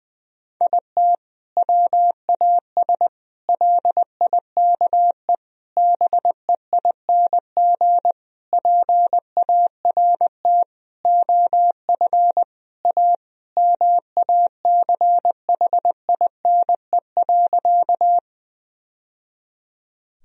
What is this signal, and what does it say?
Morse code, heard as a keyed tone, about 20 wpm, 700 Hz. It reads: IT WAS LIKE BEING PART OF A MACHINE.
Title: Sentences from Top 1000 Words - Review Part 3 - 20wpm